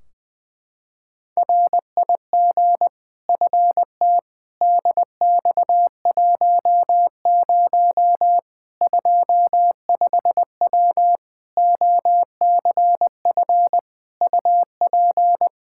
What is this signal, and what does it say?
Morse code, heard as a keyed tone, about 20 wpm, 700 Hz. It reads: RIG FT DX10 25W OCF UP
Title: Practice QSO - Exchange 2 - Info - 20wpm